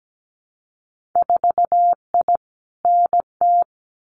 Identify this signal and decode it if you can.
Morse code, heard as a keyed tone, about 17 wpm, 700 Hz. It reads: 4I NT
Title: Winter Field Day Exchanges Encore 17wpm